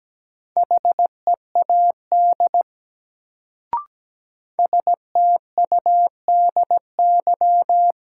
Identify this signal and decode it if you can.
Morse code, heard as a keyed tone, about 17 wpm, 700 Hz. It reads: HEAD STUDY
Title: Sentences from Top 300 Words - New Word Drill - 17wpm